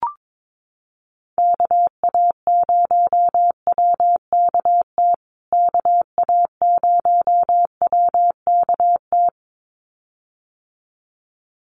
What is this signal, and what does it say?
Morse code, heard as a keyed tone, about 22 wpm, 700 Hz. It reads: KA0WKT KA0WKT
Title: USA Call Signs Repeated 2x 22wpm